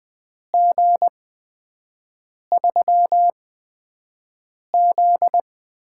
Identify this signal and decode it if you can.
Morse code, heard as a keyed tone, about 20 wpm, spaced as for 5 wpm, 700 Hz. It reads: G3Z
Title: USA Call Signs - 8x Character Spacing - 20wpm